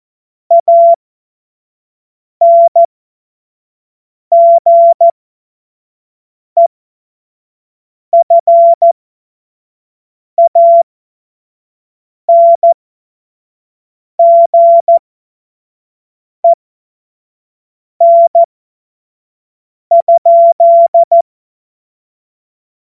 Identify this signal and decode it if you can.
Morse code, heard as a keyed tone, about 14 wpm, spaced as for 5 wpm, 680 Hz. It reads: ANGEFANGEN?
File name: Morsekurs / Witze / Witz 22. Juni 2021, 25 F70